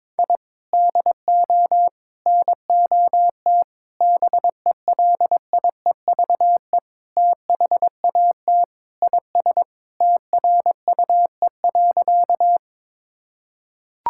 Morse code, heard as a keyed tone, about 22 wpm, 700 Hz. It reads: I DO NOT BELIEVE THAT IS TRUE.
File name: practice / 22wpm / Sentences with 4-Word Phrases - 22wpm